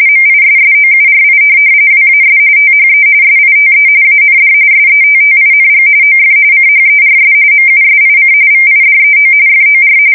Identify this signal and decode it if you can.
RTTY, about 45.45 baud. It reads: MISSOURI WAS THE SITE AND SETTING OF CHERS MUSIC VIDEO FOR
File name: fsk170 / USS-Missouri